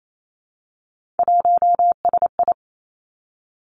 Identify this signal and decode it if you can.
Morse code, heard as a keyed tone, about 28 wpm, 700 Hz. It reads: 1SI
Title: Three Challenging Letter-Number Combinations 28wpm